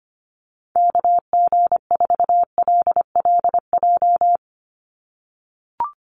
Morse code, heard as a keyed tone, about 25 wpm, 700 Hz. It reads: KG4LLJ
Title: USA Call Signs 25wpm